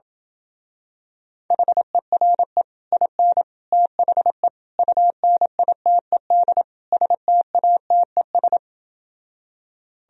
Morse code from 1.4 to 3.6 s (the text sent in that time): HERE IN